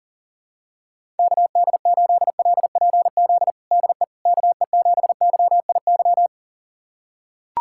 Morse code, heard as a keyed tone, about 37 wpm, 700 Hz. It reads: KD8LPZ DE KE7YIY
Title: USA Callsign DE Callsign - 6 Characters - 40wpm